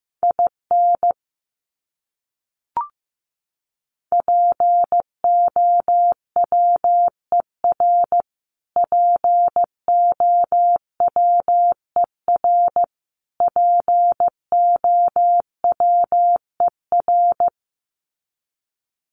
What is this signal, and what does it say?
Morse code, heard as a keyed tone, about 15 wpm, 700 Hz. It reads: IN POWER POWER POWER
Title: Top 500 Words Repeated 15wpm